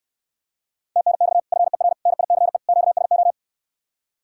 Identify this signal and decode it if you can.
Morse code, heard as a keyed tone, about 69 wpm, 700 Hz. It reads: TNX FER NICE QSO